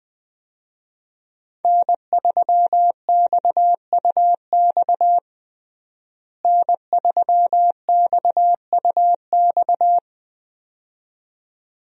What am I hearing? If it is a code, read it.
Morse code, heard as a keyed tone, about 21 wpm, 700 Hz. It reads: N3XUX N3XUX
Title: USA Call Signs - 5 Characters - Repeated 2x - 20wpm